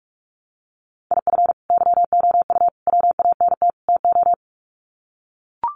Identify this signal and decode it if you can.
Morse code, heard as a keyed tone, about 44 wpm, 700 Hz. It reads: IF YOU WANT TO
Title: Sets of 4 Words 45wpm